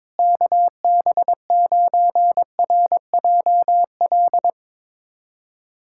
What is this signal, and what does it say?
Morse code, heard as a keyed tone, about 22 wpm, 700 Hz. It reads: KB9RJL
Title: USA Callsign DE Callsign - 6 Characters - 22wpm